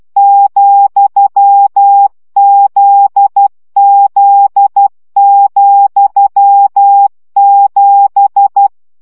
Morse code, heard as a keyed tone, about 12 wpm, 800 Hz. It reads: ,ZZ,7
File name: CW gyakorlas z 7 vesszo 10 baud